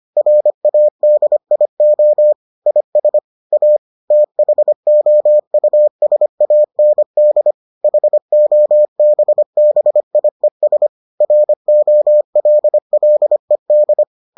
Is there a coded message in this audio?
Yes: Morse code from 0.2 to 14.0 s, heard as a keyed tone, about 25 wpm, 600 Hz. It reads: RADIO IS A THOUSAND HOBBIES ROLLED